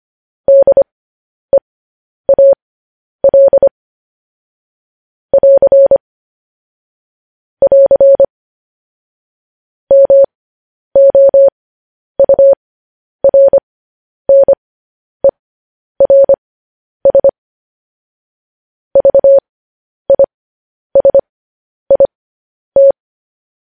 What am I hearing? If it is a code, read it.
Morse code, heard as a keyed tone, about 25 wpm, spaced as for 10 wpm, 550 Hz. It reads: DEAL <AR> <AR> MOURNERS VISIT
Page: news headlines in morse code at 10 wpm